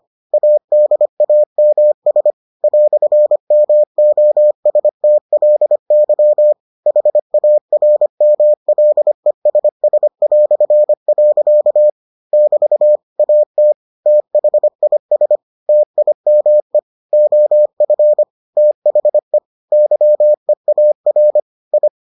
Morse code, heard as a keyed tone, about 25 wpm, 600 Hz. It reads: ADAMS "MOSTLY HARMLESS". <BT> AT THIS TIME OF THE YEAR I